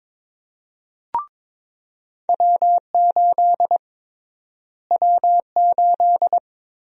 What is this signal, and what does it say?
Morse code, heard as a keyed tone, about 22 wpm, 700 Hz. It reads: W8 W8
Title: USA Call Sign Prefixes - Repeated 2x - 22wpm